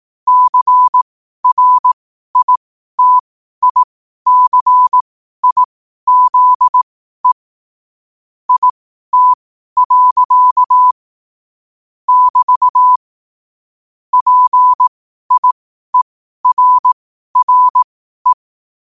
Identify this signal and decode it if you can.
Morse code, heard as a keyed tone, about 18 wpm, spaced as for 12 wpm, 1000 Hz. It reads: CRITICIZE IT. <BT> PIERRE
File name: qotdfiles / 12wpm